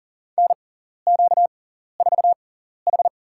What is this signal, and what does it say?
Morse code, heard as a keyed tone, about 40 wpm, spaced as for 14 wpm, 700 Hz. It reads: NQ4H
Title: USA Call Signs - 4 Characters - 6x Character Spacing - 40wpm